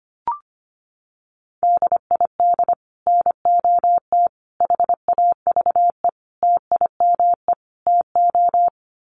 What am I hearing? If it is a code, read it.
Morse code, heard as a keyed tone, about 25 wpm, 700 Hz. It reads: DID NOT HAVE TIME TO
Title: Sets of 5 Words Encore 25wpm